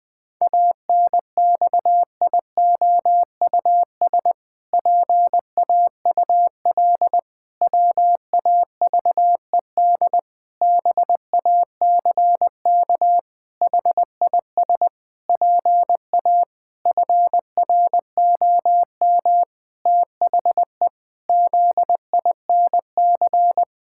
Morse code, heard as a keyed tone, about 20 wpm, 700 Hz. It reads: ANXIOUS PAUL WAVED BACK HIS PA FROM THE ZINC
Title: Pangram Challenge v2 - 20wpm